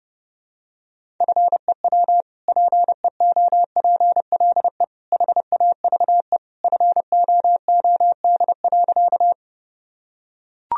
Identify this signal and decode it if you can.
Morse code, heard as a keyed tone, about 30 wpm, 700 Hz. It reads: FEW PEOPLE HAVE FOOD.
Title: Sentences from Top 1000 Words - Review Part 1 - 30wpm